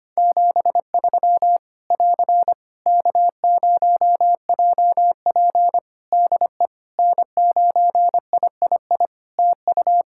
Morse code, heard as a keyed tone, about 25 wpm, 700 Hz. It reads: 73 <AR> K0JP DE N9III TU
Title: Practice QSO - Exchange 4 - Ending - 25wpm